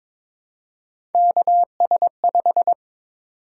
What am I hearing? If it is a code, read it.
Morse code, heard as a keyed tone, about 22 wpm, 700 Hz. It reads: KS5
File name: practice / 22wpm / Three Letter-Number Combinations - 22wpm